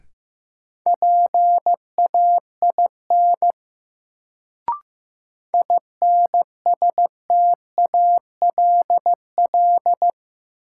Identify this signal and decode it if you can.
Morse code, heard as a keyed tone, about 15 wpm, 700 Hz. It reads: PAIN INSTALL
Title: Sentences from Top 1600 Words - New Word Drill - 15wpm